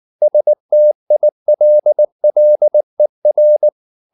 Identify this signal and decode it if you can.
Morse code, heard as a keyed tone, about 19 wpm, 600 Hz. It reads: STILLER
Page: Ferie 19 wpm